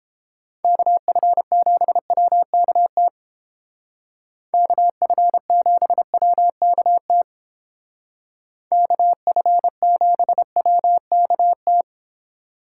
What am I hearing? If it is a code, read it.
Morse code, heard as a keyed tone, about 31 wpm, 700 Hz. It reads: KF7WKT KF7WKT KF7WKT